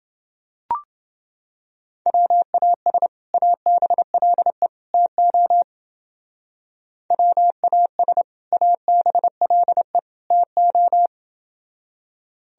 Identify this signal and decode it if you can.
Morse code, heard as a keyed tone, about 29 wpm, 700 Hz. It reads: WAS ABLE TO WAS ABLE TO